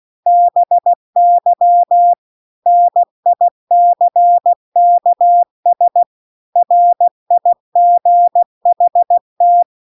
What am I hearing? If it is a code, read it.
Morse code, heard as a keyed tone, about 16 wpm, 700 Hz. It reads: BY NICKS RIGHT
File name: CW-The-Big-Twoheaded-River-2-16WPM